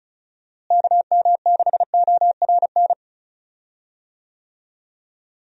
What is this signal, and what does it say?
Morse code, heard as a keyed tone, about 36 wpm, 700 Hz. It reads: KM6ORN
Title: USA Call Signs - 6 Characters - Repeated 2x - 35wpm